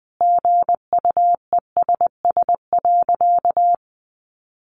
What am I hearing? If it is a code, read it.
Morse code, heard as a keyed tone, about 20 wpm, 700 Hz. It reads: GUESS.
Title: Sentences from Top 1500 Words 20wpm